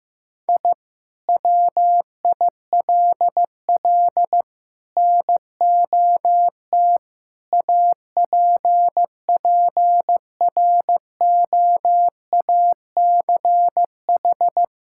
Morse code, heard as a keyed tone, about 15 wpm, 700 Hz. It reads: I WILL NOT APPROACH
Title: Sentences from Top 1000 Words - Review Part 6 - 15wpm